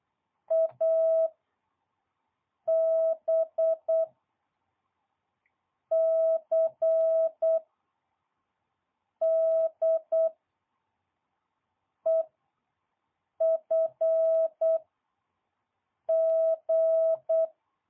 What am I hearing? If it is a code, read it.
Morse code, heard as a keyed tone, about 8 wpm, spaced as for 4 wpm, 650 Hz. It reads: AB CDEFG
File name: jh8cbh / 2022cw